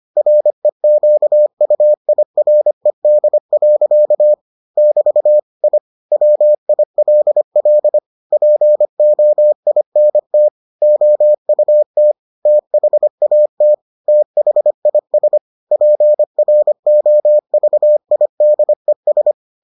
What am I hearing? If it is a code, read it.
Morse code, heard as a keyed tone, about 25 wpm, 600 Hz. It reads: REQUIRED. <BT> I WILL POINT OUT THAT THIS PROVIDES